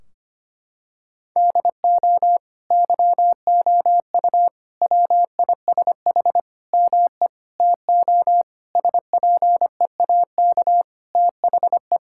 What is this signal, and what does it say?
Morse code, heard as a keyed tone, about 25 wpm, 700 Hz. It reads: DO YOU WISH ME TO SPEAK THE
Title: Sentences from Top 1600 Words 25wpm